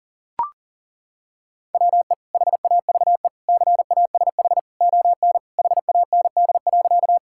Morse code, heard as a keyed tone, about 37 wpm, 700 Hz. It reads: WE HAVE CASH ON HAND.
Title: Sentences from Top 1000 Words - Review Part 5 - 40wpm